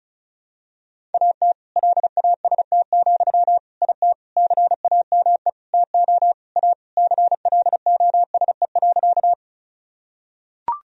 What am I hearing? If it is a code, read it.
Morse code, heard as a keyed tone, about 36 wpm, 700 Hz. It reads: AT LAST, IT CAME TO A CLOSE.
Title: Sentences from Top 1000 Words - Review Part 1 - 35wpm